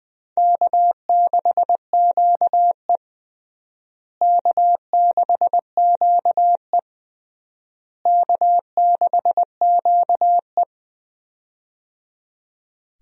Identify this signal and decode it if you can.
Morse code, heard as a keyed tone, about 21 wpm, 700 Hz. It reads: K6QE K6QE K6QE
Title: USA Call Signs - 4 Characters - Repeated 3x - 20wpm